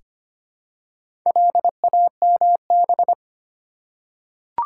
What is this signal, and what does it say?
Morse code, heard as a keyed tone, about 25 wpm, 690 Hz. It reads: LAMB
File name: practice / 25wpm / Easy Words with Hints - 25wpm